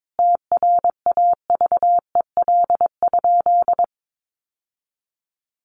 Morse code, heard as a keyed tone, about 22 wpm, 700 Hz. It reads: TRAVEL?
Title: Sentences from Top 1400 Words 22wpm